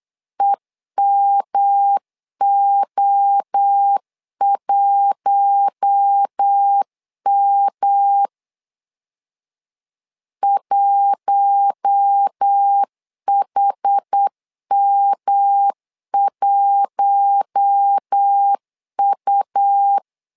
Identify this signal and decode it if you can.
Morse code, heard as a keyed tone, about 8 wpm, 800 Hz. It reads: EMO1M 1HM1U
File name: LEZIONI / 4 Q1 8wpm 3035